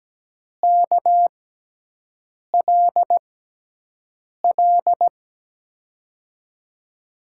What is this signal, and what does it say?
Morse code, heard as a keyed tone, about 17 wpm, spaced as for 6 wpm, 700 Hz. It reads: KLL
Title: USA Call Signs - 6 Characters - 6x Character Spacing - 17wpm